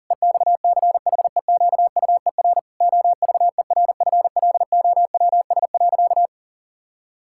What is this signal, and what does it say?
Morse code, heard as a keyed tone, about 40 wpm, 700 Hz. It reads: EXCHEQUER OVERFLOWS.
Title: Pangram Challenge v2 - 40wpm